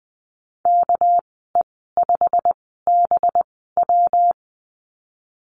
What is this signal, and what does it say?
Morse code, heard as a keyed tone, about 20 wpm, spaced as for 15 wpm, 700 Hz. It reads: KE5BW
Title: 200 Lesson 32g - Letter k - Call Signs 20wpm